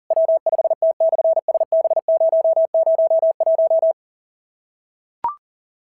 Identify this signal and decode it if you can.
Morse code, heard as a keyed tone, about 40 wpm, 650 Hz. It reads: W5T/SB001
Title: SOTA QSO Practice 40wpm